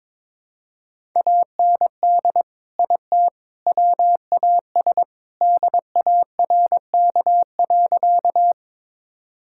Morse code, heard as a keyed tone, about 22 wpm, 700 Hz. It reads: AND IT WAS DARK.
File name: practice / 22wpm / Sentences with 3-Word Phrases - 22wpm